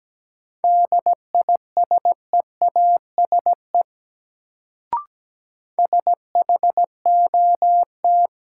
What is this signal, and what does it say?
Morse code, heard as a keyed tone, about 17 wpm, 700 Hz. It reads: DISEASE SHOT
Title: Sentences from Top 1100 Words - New Word Drill Alt - 17wpm